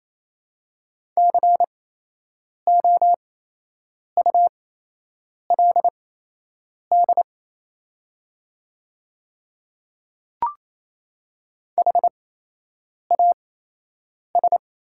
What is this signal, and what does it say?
Morse code, heard as a keyed tone, about 28 wpm, spaced as for 8 wpm, 700 Hz. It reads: COULD HAS
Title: 8x Character Spacing - Top 100 Words - 28wpm